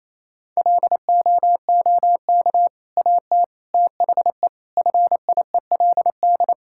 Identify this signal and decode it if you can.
Morse code, heard as a keyed tone, about 28 wpm, 700 Hz. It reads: LOOK AT THE FIELD
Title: Sentences from Top 1000 Words - Review Part 5 - 28wpm